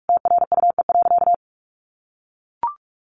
Morse code, heard as a keyed tone, about 44 wpm, 700 Hz. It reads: TRUE.
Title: Sentences from Top 1400 Words 45wpm